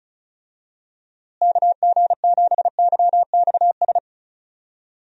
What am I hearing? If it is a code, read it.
Morse code, heard as a keyed tone, about 34 wpm, 700 Hz. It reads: KG7YXS
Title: USA Call Signs - 6 Characters - Repeated 3x - 35wpm